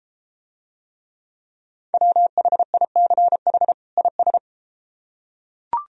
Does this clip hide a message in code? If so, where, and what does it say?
Yes: Morse code from 1.9 to 4.4 s, heard as a keyed tone, about 32 wpm, 700 Hz. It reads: WHICH IS